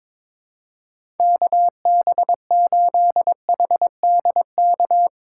Morse code, heard as a keyed tone, about 22 wpm, 700 Hz. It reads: KB8HDK
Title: USA Call Signs - 6 Characters - 2x Character Spacing - 22wpm